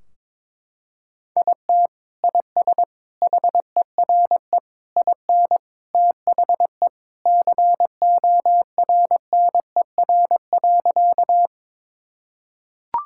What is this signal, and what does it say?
Morse code, heard as a keyed tone, about 22 wpm, 700 Hz. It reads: IT IS HERE IN THE CORNER.
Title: Sentences from Top 1600 Words 22wpm